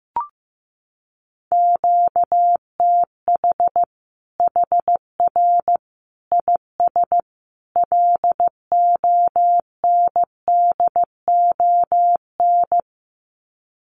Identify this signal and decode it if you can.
Morse code, heard as a keyed tone, about 15 wpm, 700 Hz. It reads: QTH HR IS LONDON
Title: Common QSO Phrases 15wpm